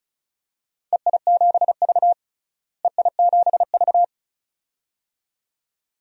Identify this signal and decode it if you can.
Morse code, heard as a keyed tone, about 36 wpm, 690 Hz. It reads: EI7V EI7V
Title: DXCC-100 Call Signs - 4 Characters - Repeated 2x - 35wpm